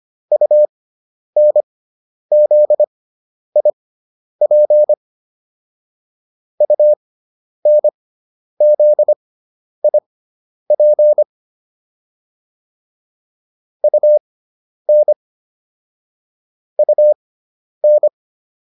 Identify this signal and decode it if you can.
Morse code, heard as a keyed tone, about 25 wpm, spaced as for 10 wpm, 600 Hz. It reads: UNZIP UNZIP UN UN